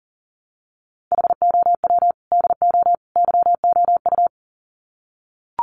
Morse code, heard as a keyed tone, about 40 wpm, 700 Hz. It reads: HOW DO YOU
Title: Sets of 3 Words 40wpm